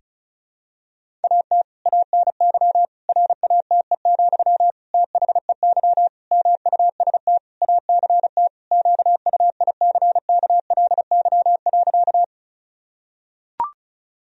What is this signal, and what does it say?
Morse code, heard as a keyed tone, about 36 wpm, 700 Hz. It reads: AT ANY RATE, THEY MUST ACT QUICKLY.
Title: Sentences from Top 1000 Words - Review Part 4 - 35wpm